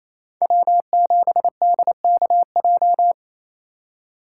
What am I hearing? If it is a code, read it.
Morse code, heard as a keyed tone, about 33 wpm, 700 Hz. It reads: W7DKJ
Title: Speed Racing - USA 1x3 Call Signs 28wpm